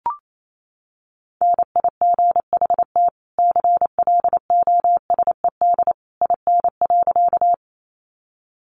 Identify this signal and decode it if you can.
Morse code, heard as a keyed tone, about 28 wpm, 700 Hz. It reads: NIGHT CLOSED IN.
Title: Sentences from Top 1400 Words 28wpm